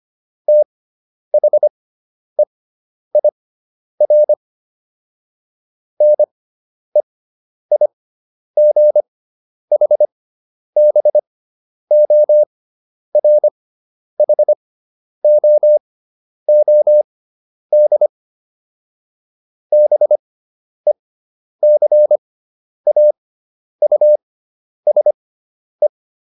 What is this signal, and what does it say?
Morse code, heard as a keyed tone, about 25 wpm, spaced as for 10 wpm, 600 Hz. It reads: THEIR NEIGHBORHOOD BECAUSE